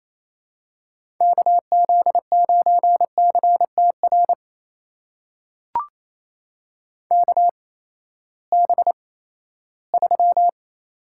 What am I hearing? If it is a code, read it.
Morse code, heard as a keyed tone, about 28 wpm, spaced as for 16 wpm, 700 Hz. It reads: KZ9CTR KB3
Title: USA Call Signs - 6 Characters - 8x Character Spacing - 28wpm